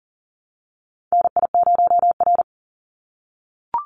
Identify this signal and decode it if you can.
Morse code, heard as a keyed tone, about 40 wpm, 700 Hz. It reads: NI0R
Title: Call Signs - 40wpm